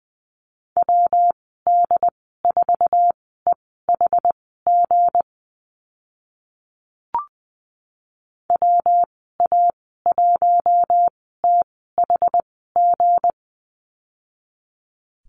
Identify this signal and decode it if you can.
Morse code, heard as a keyed tone, about 20 wpm, spaced as for 14 wpm, 700 Hz. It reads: WD4EHG WA1THG
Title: 173 Lesson 28g - Number 9 - Call Signs 20wpm